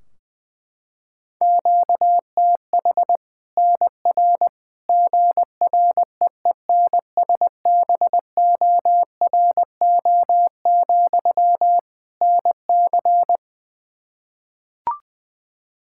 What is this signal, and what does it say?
Morse code, heard as a keyed tone, about 20 wpm, 700 Hz. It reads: QTH NR GREENSBORO, NC
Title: Practice QSO - Exchange 1 - Beginning - 20wpm